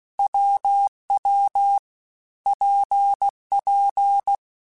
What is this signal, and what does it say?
Morse code, heard as a keyed tone, about 16 wpm, 800 Hz. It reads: WW PP
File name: Difficult Combos / Difficult Combos-mp3s / WPJ WPJ1 @15